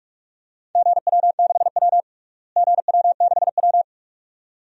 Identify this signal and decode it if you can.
Morse code, heard as a keyed tone, about 46 wpm, 700 Hz. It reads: GW6W GW6W
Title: DXCC-100 Call Signs - 4 Characters - Repeated 2x - 45wpm